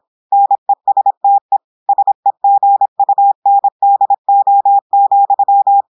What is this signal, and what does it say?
Morse code, heard as a keyed tone, about 26 wpm, 800 Hz. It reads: NESTE SEGUNDO,
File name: audios / portugais / portugais texte en clair 1 vitesse 26 mots minutes